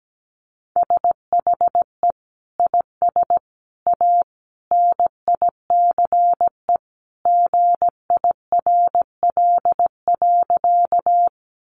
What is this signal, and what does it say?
Morse code, heard as a keyed tone, about 17 wpm, 700 Hz. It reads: SHE IS A NICE GIRL.
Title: Sentences from Top 1100 Words 17wpm